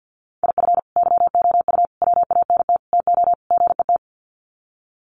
Morse code, heard as a keyed tone, about 49 wpm, 700 Hz. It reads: IF YOU WANT TO GET
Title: Sets of 5 Words Encore 50wpm